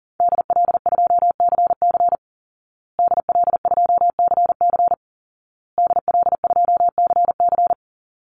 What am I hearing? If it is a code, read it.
Morse code, heard as a keyed tone, about 40 wpm, 700 Hz. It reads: DL2CC DL2CC DL2CC
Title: Call Signs Repeated 2x - 40wpm